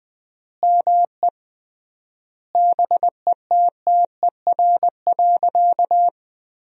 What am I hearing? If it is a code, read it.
Morse code, heard as a keyed tone, about 20 wpm, 700 Hz. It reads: ME BETTER.
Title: 3x Word-Spacing - Sentences from Top 400 Words - 20wpm